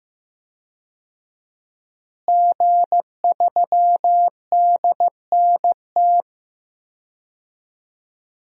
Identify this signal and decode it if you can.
Morse code, heard as a keyed tone, about 16 wpm, 700 Hz. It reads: G3DNT
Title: DXCC-100 Call Signs - 5 Characters - Repeated 3x - 15wpm